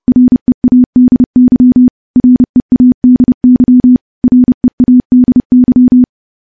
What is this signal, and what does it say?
Morse code, heard as a keyed tone, about 30 wpm, 250 Hz. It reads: READY READY READY
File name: morse / morse-training-30WPM